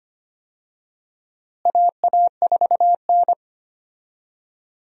Morse code, heard as a keyed tone, about 25 wpm, 700 Hz. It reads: AA4N
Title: USA Call Signs - 4 Characters - Repeated 3x - 25wpm